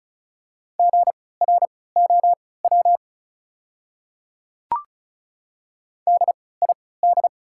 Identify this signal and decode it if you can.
Morse code, heard as a keyed tone, about 36 wpm, spaced as for 20 wpm, 690 Hz. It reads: GROW DID
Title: 3x Character Spacing - Top 300 Words - 35wpm